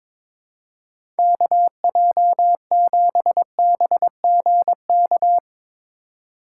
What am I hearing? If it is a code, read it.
Morse code, heard as a keyed tone, about 22 wpm, 700 Hz. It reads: KJ7BGK
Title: USA Call Signs - 6 Characters - 4x Character Spacing - 22wpm